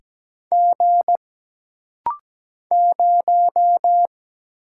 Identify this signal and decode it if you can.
Morse code, heard as a keyed tone, about 17 wpm, 700 Hz. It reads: G 0
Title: Single Letter-Number Alt - Rapid-Fire 17wpm